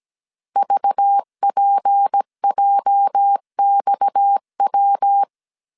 Morse code, heard as a keyed tone, about 17 wpm, 800 Hz. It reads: VPJXW
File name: CW / DET / 1 9 1500C S C 16wpm 2909